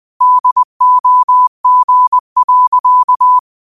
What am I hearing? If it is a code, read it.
Morse code, heard as a keyed tone, about 20 wpm, 1000 Hz. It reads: DOG.
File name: qotdfiles / 20wpm